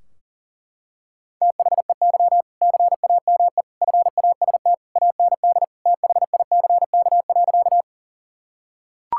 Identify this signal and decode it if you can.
Morse code, heard as a keyed tone, about 40 wpm, 700 Hz. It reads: THEY CAME FAST AND THICK.
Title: Sentences from Top 1600 Words 40wpm